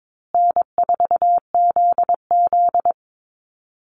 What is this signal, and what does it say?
Morse code, heard as a keyed tone, about 22 wpm, 700 Hz. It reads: N4ZZ
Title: Call Signs Repeated 3x - 22wpm